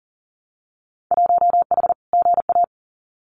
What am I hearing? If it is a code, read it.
Morse code, heard as a keyed tone, about 40 wpm, 700 Hz. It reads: 1H GA
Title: Winter Field Day Exchanges 40wpm